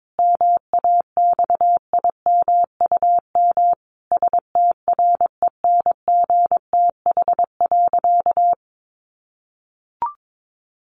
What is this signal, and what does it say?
Morse code, heard as a keyed tone, about 22 wpm, 700 Hz. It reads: MAXIMUM STRENGTH.
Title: Sentences from Top 1300 Words 22wpm